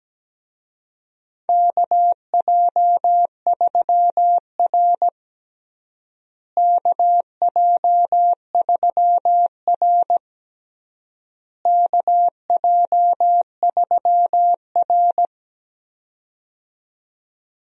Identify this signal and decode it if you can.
Morse code, heard as a keyed tone, about 17 wpm, 700 Hz. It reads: KJ3R KJ3R KJ3R
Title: USA Call Signs - 4 Characters - Repeated 3x - 17wpm